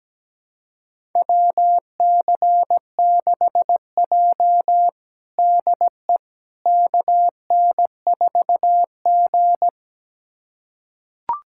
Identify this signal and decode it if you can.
Morse code, heard as a keyed tone, about 17 wpm, 700 Hz. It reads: WC6J DE KN4G
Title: USA Callsign DE Callsign - 4 Characters - 17wpm